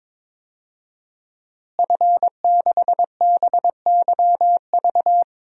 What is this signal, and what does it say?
Morse code, heard as a keyed tone, about 22 wpm, 700 Hz. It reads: F6BYV
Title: DXCC-100 Call Signs - 5 Characters - Repeated 3x - 22wpm